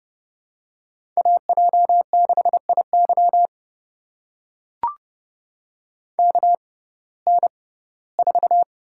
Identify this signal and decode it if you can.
Morse code, heard as a keyed tone, about 30 wpm, spaced as for 18 wpm, 700 Hz. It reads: AJ6IY KN4
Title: USA Call Signs - 5 Characters - 6x Character Spacing - 30wpm